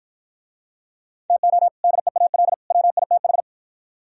Morse code, heard as a keyed tone, about 52 wpm, 700 Hz. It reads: TO DEAL WITH